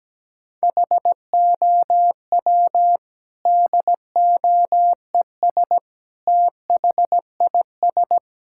Morse code, heard as a keyed tone, about 17 wpm, 700 Hz. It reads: HOW DOES THIS
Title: Sentences from Top 1000 Words - Review Part 5 - 17wpm